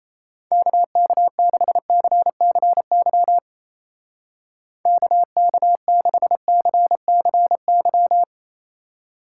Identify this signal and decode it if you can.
Morse code, heard as a keyed tone, about 34 wpm, 700 Hz. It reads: KK6CCY KK6CCY
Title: Speed Racing - USA 2x3 Call Signs 28wpm